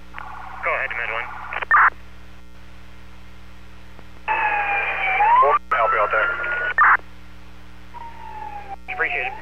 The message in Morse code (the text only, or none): none